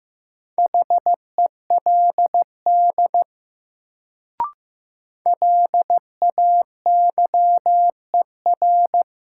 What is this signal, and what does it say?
Morse code, heard as a keyed tone, about 15 wpm, 700 Hz. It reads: HELD LAYER
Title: Sentences from Top 1600 Words - New Word Drill Alt - 15wpm